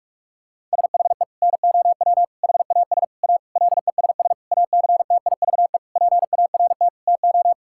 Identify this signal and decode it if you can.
Morse code, heard as a keyed tone, about 46 wpm, 700 Hz. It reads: SHE NOW HAS A LESS ACTIVE PART TO
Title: Sentences from Top 1000 Words - Review Part 6 - 45wpm